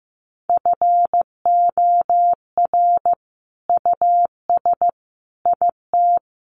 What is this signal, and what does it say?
Morse code, heard as a keyed tone, about 15 wpm, 700 Hz. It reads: FOR US IT
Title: Sentences from Top 1200 Words 15wpm